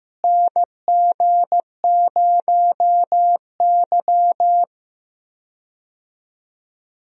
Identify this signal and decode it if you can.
Morse code, heard as a keyed tone, about 15 wpm, 700 Hz. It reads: NG0Y
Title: USA Call Signs - 4 Characters - Repeated 2x - 15wpm